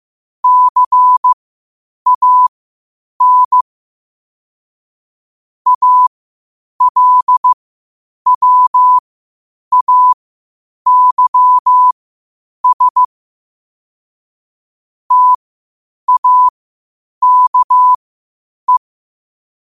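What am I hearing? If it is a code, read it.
Morse code, heard as a keyed tone, about 15 wpm, spaced as for 8 wpm, 1000 Hz. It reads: CAN ALWAYS TAKE
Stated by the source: Morse code at 8 wpm.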